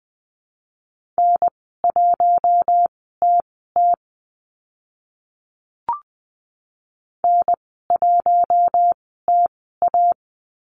Morse code, heard as a keyed tone, about 20 wpm, spaced as for 14 wpm, 700 Hz. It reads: N1TT N1TA
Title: 055 Lesson 10g - Letter r - Call Signs 20wpm